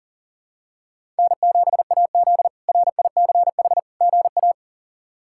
Morse code, heard as a keyed tone, about 37 wpm, 700 Hz. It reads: N7AZ RICH GA
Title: K1USN SST QSO Practice - 40wpm